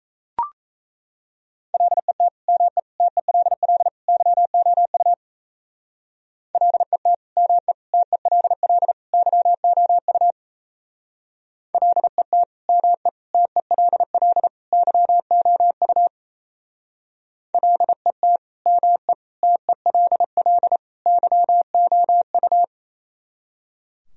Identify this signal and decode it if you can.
Morse code, heard as a keyed tone, about 34 wpm, 700 Hz. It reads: LET ME TELL YOU LET ME TELL YOU LET ME TELL YOU LET ME TELL YOU
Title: Speed-Racing - Sets of 4 Words 28wpm